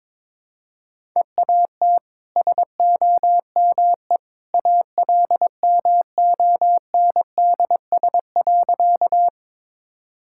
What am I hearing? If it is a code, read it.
Morse code, heard as a keyed tone, about 21 wpm, 700 Hz. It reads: EAT SOME ALMONDS.
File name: practice / 22wpm / Easy Sentences with Hints - 22wpm